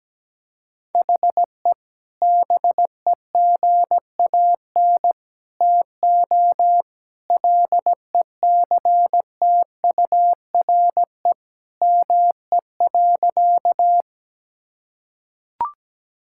Morse code, heard as a keyed tone, about 17 wpm, 700 Hz. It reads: HE BEGAN TO LECTURE ME.
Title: Sentences from Top 1600 Words 17wpm